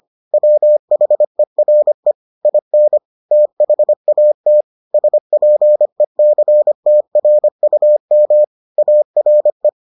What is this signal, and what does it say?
Morse code, heard as a keyed tone, about 25 wpm, 600 Hz. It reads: WHERE IN THAT SPECTRUM ARE